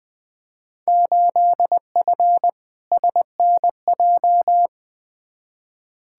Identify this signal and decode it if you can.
Morse code, heard as a keyed tone, about 20 wpm, 700 Hz. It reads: 8F SNJ
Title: Field Day Exchanges Encore - 20wpm